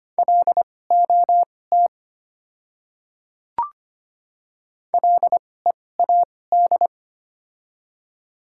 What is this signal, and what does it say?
Morse code, heard as a keyed tone, about 25 wpm, spaced as for 18 wpm, 700 Hz. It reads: LOT LEAD
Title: 2x Character Spacing - Top 500 Words - 25wpm